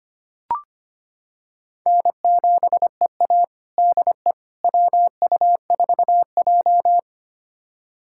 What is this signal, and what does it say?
Morse code, heard as a keyed tone, about 25 wpm, 700 Hz. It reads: N7EA DE WU4J
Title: USA Callsign DE Callsign - 4 Characters - 25wpm